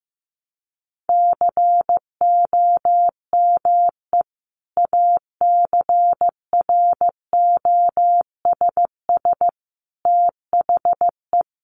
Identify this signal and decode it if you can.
Morse code, heard as a keyed tone, about 15 wpm, 700 Hz. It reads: COME ACROSS THE
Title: Sentences from Top 700 Words 15wpm